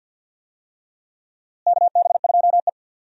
Morse code, heard as a keyed tone, about 49 wpm, 700 Hz. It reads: KB2E
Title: USA Call Signs - 4 Characters - Repeated 3x - 50wpm